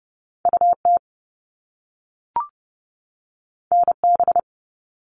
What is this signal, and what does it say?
Morse code, heard as a keyed tone, about 30 wpm, 700 Hz. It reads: UT NB
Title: All Cut Numbers - 2-Digit 30wpm